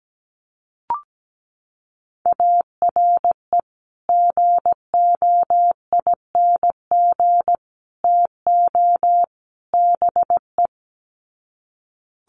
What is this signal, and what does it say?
Morse code, heard as a keyed tone, about 17 wpm, 700 Hz. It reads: ARE GOING TO BE
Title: Sets of 4 Words 17wpm